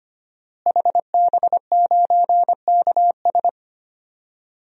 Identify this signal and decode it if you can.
Morse code, heard as a keyed tone, about 25 wpm, 700 Hz. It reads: HB9KS
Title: DXCC-100 Call Signs - 5 Characters - Repeated 2x - 25wpm